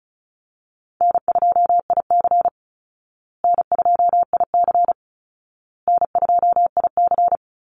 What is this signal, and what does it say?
Morse code, heard as a keyed tone, about 34 wpm, 700 Hz. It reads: N2IC N2IC N2IC
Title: Easy Call Signs Repeated 3x - 35wpm